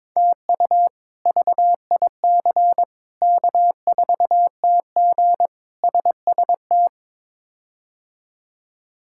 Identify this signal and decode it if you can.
Morse code, heard as a keyed tone, about 22 wpm, 700 Hz. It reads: TU VIC K4TG SST
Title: K1USN SST QSO Practice - 22wpm